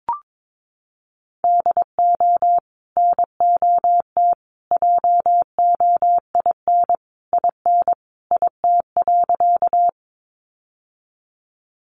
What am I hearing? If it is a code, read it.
Morse code, heard as a keyed tone, about 22 wpm, 700 Hz. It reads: DO NOT JOIN IN IT.
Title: Sentences from Top 1400 Words 22wpm